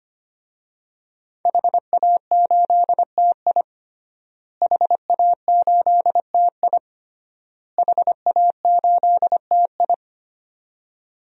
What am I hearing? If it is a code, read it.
Morse code, heard as a keyed tone, about 25 wpm, 700 Hz. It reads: HA8TI HA8TI HA8TI
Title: DXCC-100 Call Signs - 5 Characters - Repeated 3x - 25wpm